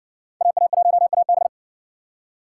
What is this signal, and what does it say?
Morse code, heard as a keyed tone, about 60 wpm, 700 Hz. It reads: AA1AB